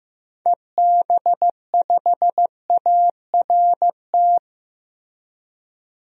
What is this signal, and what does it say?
Morse code, heard as a keyed tone, about 15 wpm, 700 Hz. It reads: EB5ART
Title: DXCC-100 Call Signs - 6 Characters - 15wpm